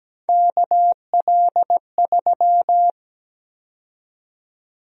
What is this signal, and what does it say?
Morse code, heard as a keyed tone, about 17 wpm, 700 Hz. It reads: KL3
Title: USA Call Sign Prefixes - Repeated 2x - 17wpm